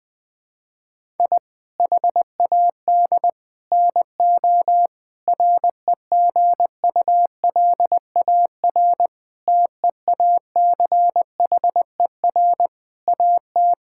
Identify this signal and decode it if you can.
Morse code, heard as a keyed tone, about 21 wpm, 700 Hz. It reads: I HAD NO REGULAR TEACHER AT
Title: Sentences from Top 1000 Words - Review Part 6 - 20wpm